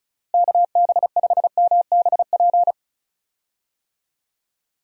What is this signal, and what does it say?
Morse code, heard as a keyed tone, about 36 wpm, 700 Hz. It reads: KB5MBP
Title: USA Call Signs - 6 Characters - Repeated 2x - 35wpm